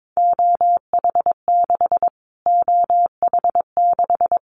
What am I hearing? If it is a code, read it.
Morse code, heard as a keyed tone, about 22 wpm, 700 Hz. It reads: OH6 OH6
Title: Callsign Prefixes for Top 25 DXCC - Repeated 3x - 22wpm